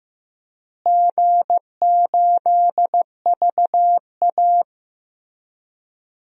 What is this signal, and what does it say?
Morse code, heard as a keyed tone, about 15 wpm, 700 Hz. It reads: G8VA
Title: DXCC-100 Call Signs - 4 Characters - Repeated 3x - 15wpm